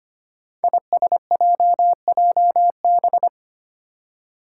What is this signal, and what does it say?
Morse code, heard as a keyed tone, about 25 wpm, 700 Hz. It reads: ISJJB
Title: Five Challenging Letter-Number Combinations 25wpm